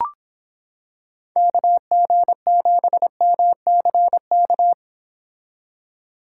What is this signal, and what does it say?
Morse code, heard as a keyed tone, about 26 wpm, 700 Hz. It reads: KG7MCK